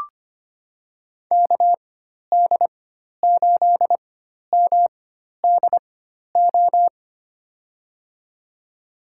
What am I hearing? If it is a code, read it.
Morse code, heard as a keyed tone, about 25 wpm, spaced as for 12 wpm, 690 Hz. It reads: KD8MDO
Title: USA Call Signs - 4x Character Spacing - 25wpm